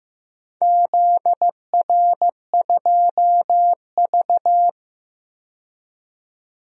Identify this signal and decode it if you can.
Morse code, heard as a keyed tone, about 15 wpm, 700 Hz. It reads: ZR2V
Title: DXCC-100 Call Signs - 4 Characters - 15wpm